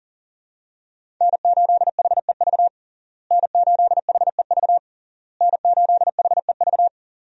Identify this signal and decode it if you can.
Morse code, heard as a keyed tone, about 40 wpm, 700 Hz. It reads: N8HEV N8HEV N8HEV
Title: USA Call Signs - 5 Characters - Repeated 3x - 40wpm